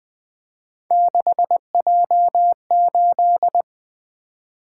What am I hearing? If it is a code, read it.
Morse code, heard as a keyed tone, about 21 wpm, 700 Hz. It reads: 6J8
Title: Three Letter-Number Combinations - 20wpm